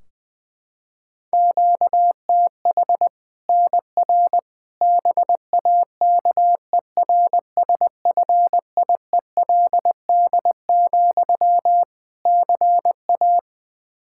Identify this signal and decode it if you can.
Morse code, heard as a keyed tone, about 21 wpm, 700 Hz. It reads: QTH NR BAKERSFIELD, CA
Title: K1USN SST QSO Practice 20wpm